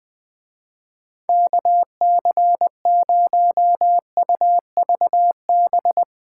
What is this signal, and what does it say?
Morse code, heard as a keyed tone, about 22 wpm, 700 Hz. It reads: KC0UVB
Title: Speed Racing - USA 2x3 Call Signs 20wpm